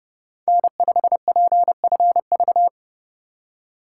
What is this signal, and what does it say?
Morse code, heard as a keyed tone, about 31 wpm, 700 Hz. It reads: N5PFV
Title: USA Call Signs - 2x Character Spacing - 30wpm